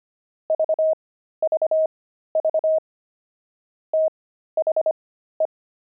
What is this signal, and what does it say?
Morse code, heard as a keyed tone, about 25 wpm, spaced as for 13 wpm, 630 Hz. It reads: VVV THE
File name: pangram-fs-13-cs-25-zeal